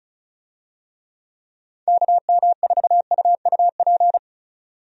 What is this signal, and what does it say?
Morse code, heard as a keyed tone, about 36 wpm, 700 Hz. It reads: KM4UUP
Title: USA Call Signs - 6 Characters - 35wpm